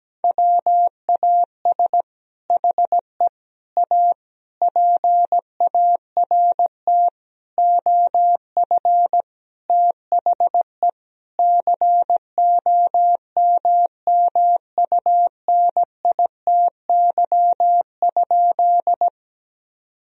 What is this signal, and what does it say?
Morse code, heard as a keyed tone, about 17 wpm, 700 Hz. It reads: WAS HE A PART OF THE COMMUNITY?
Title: Sentences from Top 1000 Words - Review Part 4 - 17wpm